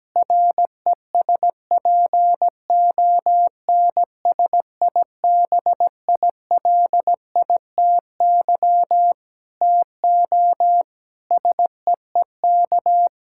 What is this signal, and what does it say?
Morse code, heard as a keyed tone, about 17 wpm, 700 Hz. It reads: RESPONSIBILITY TO SEEK
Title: Sentences from Top 1600 Words 17wpm